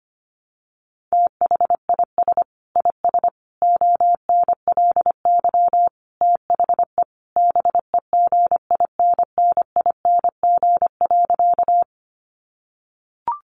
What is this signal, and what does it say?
Morse code, heard as a keyed tone, about 25 wpm, 700 Hz. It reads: THIS IS ONLY THE BEGINNING.
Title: Sentences from Top 800 Words 25wpm